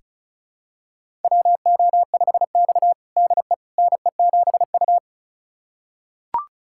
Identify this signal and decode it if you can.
Morse code, heard as a keyed tone, about 36 wpm, 690 Hz. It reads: WO5X DE NE7U
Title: USA Callsign DE Callsign - 4 Characters - 35wpm